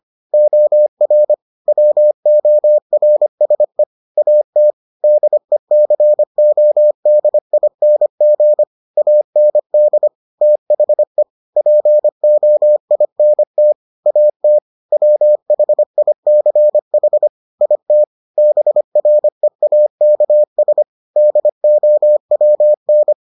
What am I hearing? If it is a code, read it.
Morse code, heard as a keyed tone, about 25 wpm, 600 Hz. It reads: OR WORSE AT DECODING AND THE POINT AT WHICH IT BREAKS DOWN